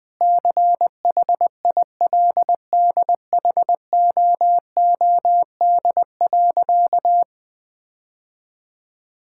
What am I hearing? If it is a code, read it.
Morse code, heard as a keyed tone, about 20 wpm, 700 Hz. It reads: CHILDHOOD.
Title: Sentences with 2-Word Phrases - Spoken Beginning - 20wpm